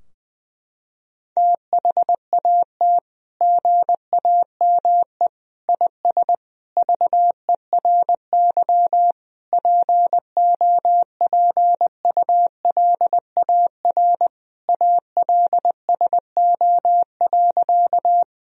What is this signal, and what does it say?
Morse code, heard as a keyed tone, about 20 wpm, 700 Hz. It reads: THAT GAME IS VERY POPULAR ALSO.
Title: Sentences from Top 1000 Words - Review Part 3 - 20wpm